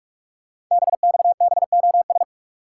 Morse code, heard as a keyed tone, about 44 wpm, 700 Hz. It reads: BXBOS
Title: Code Groups - 5 Letters - 45wpm